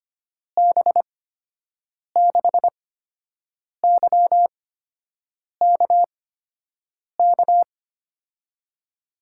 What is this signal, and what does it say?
Morse code, heard as a keyed tone, about 25 wpm, spaced as for 7 wpm, 700 Hz. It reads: B6YKK
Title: USA Call Signs - 6 Characters - 8x Character Spacing - 25wpm